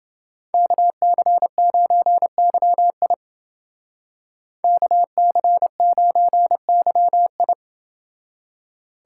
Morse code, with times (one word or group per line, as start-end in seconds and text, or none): 0.5-3.1 KC9YI
4.6-7.5 KC9YI